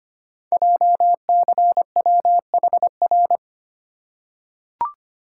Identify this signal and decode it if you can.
Morse code, heard as a keyed tone, about 25 wpm, 700 Hz. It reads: JCWHR
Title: Code Groups - 5 Letters - 25wpm